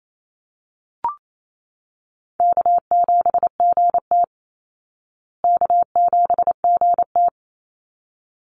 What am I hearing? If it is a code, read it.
Morse code, heard as a keyed tone, about 28 wpm, 700 Hz. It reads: K7GT K7GT
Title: Easy Call Signs Repeated 2x - 28wpm